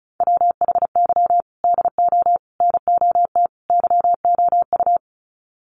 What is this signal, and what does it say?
Morse code, heard as a keyed tone, about 34 wpm, 700 Hz. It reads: WHY DO NOT YOU
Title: Sets of 4 Words 35wpm